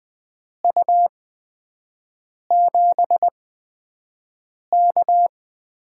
Morse code, heard as a keyed tone, about 20 wpm, spaced as for 5 wpm, 700 Hz. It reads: U7K
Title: USA Call Signs - 4 Characters - 8x Character Spacing - 20wpm